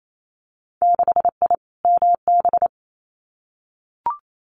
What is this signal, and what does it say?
Morse code, heard as a keyed tone, about 28 wpm, 700 Hz. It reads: 6I MB
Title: Winter Field Day Exchanges Encore 28wpm